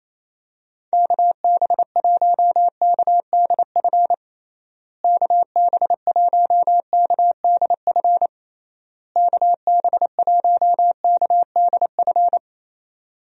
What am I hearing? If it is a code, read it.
Morse code, heard as a keyed tone, about 28 wpm, 700 Hz. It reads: KB1KDF KB1KDF KB1KDF
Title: USA Call Signs - 6 Characters - Repeated 3x - 28wpm